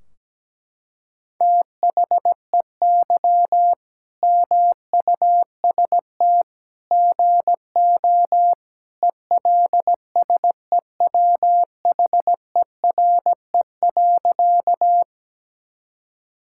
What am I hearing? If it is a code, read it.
Morse code, heard as a keyed tone, about 17 wpm, 700 Hz. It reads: THEY MUST GO ELSEWHERE.
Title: Sentences from Top 1600 Words 17wpm